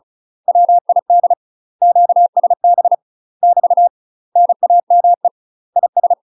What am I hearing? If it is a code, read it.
Morse code, heard as a keyed tone, about 34 wpm, 700 Hz. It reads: WID QSB <BT> NAME IS
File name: UK MASTER QSOs TRAINING / 35WPM / UK MASTER QSOs 0039